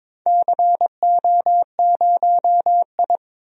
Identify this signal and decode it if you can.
Morse code, heard as a keyed tone, about 22 wpm, 700 Hz. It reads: CO0I
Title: DXCC-100 Call Signs - 4 Characters - 22wpm